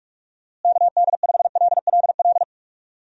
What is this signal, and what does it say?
Morse code, heard as a keyed tone, about 44 wpm, 700 Hz. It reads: KD5LLL
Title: USA Call Signs - 6 Characters - Repeated 3x - 45wpm